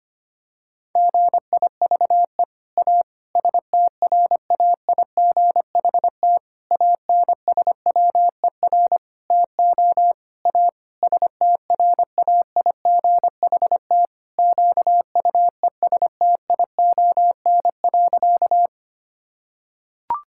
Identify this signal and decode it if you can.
Morse code, heard as a keyed tone, about 25 wpm, 700 Hz. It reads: GIVE A STRAIGHT ANSWER TO A STRAIGHT QUESTION.
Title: Sentences from Top 1000 Words - Review Part 6 - 25wpm